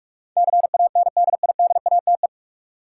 Morse code, heard as a keyed tone, about 46 wpm, 700 Hz. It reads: CANDIDATE
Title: Sentences from Top 1600 Words - New Word Drill - 45wpm